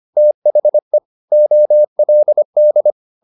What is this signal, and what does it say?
Morse code, heard as a keyed tone, about 25 wpm, 600 Hz. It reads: THE OLD
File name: the old man and the sea part 1 25wpm